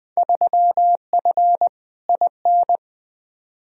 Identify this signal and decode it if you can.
Morse code, heard as a keyed tone, about 20 wpm, 700 Hz. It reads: 3F IN
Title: Field Day Exchanges - 20wpm